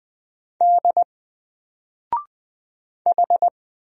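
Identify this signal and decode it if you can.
Morse code, heard as a keyed tone, about 20 wpm, 700 Hz. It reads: D H
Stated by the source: Single letter and number Morse drill at 20 wpm, from N4LCD.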